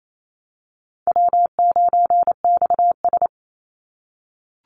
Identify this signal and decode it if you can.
Morse code, heard as a keyed tone, about 28 wpm, 700 Hz. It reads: W9XS
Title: Easy Call Signs - 28wpm